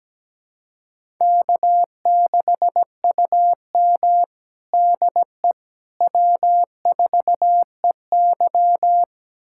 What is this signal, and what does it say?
Morse code, heard as a keyed tone, about 17 wpm, 700 Hz. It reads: K6UM DE W4EY
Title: USA Callsign DE Callsign - 4 Characters - 17wpm